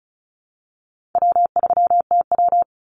Morse code, heard as a keyed tone, about 34 wpm, 700 Hz. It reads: W3TW
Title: Call Signs - 35wpm